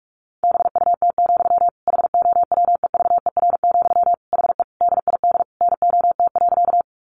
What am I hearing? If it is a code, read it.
Morse code, heard as a keyed tone, about 44 wpm, 700 Hz. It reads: BUT, HOWEVER, HE DID NOT.
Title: Sentences from Top 600 Words 45wpm